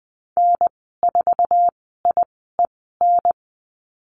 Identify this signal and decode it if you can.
Morse code, heard as a keyed tone, about 20 wpm, spaced as for 15 wpm, 700 Hz. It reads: N4IEN
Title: 140 Lesson 23g - Letter f - Call Signs 20wpm